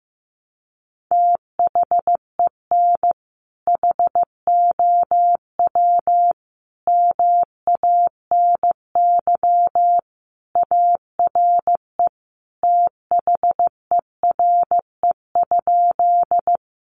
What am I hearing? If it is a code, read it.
Morse code, heard as a keyed tone, about 15 wpm, 700 Hz. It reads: THEN HOW MANY ARE THERE?
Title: Sentences from Top 100 Words 15wpm